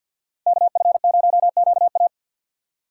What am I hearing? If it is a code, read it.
Morse code, heard as a keyed tone, about 52 wpm, 700 Hz. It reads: KF0QA
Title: USA Call Signs - 5 Characters - 6x Character Spacing - 50wpm